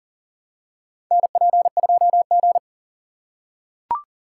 Morse code, heard as a keyed tone, about 40 wpm, 700 Hz. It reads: NP2G
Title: USA Call Signs - 2x Character Spacing - 40wpm